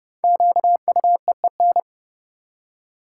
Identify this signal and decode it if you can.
Morse code, heard as a keyed tone, about 30 wpm, 700 Hz. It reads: QUEEN
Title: Five Letter Words - 30wpm